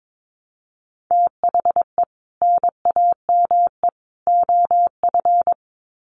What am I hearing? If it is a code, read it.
Morse code, heard as a keyed tone, about 22 wpm, 700 Hz. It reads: THE NAME OF
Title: Sets of 3 Words 22wpm